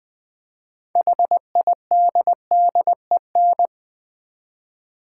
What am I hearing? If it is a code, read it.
Morse code, heard as a keyed tone, about 20 wpm, 700 Hz. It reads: HIDDEN
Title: 6 Letter Words - First 3 Letters Given - 20wpm